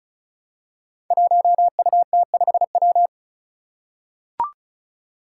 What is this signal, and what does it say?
Morse code, heard as a keyed tone, about 36 wpm, 700 Hz. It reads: 1UT5W
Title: Five Challenging Letter-Number Combinations 35wpm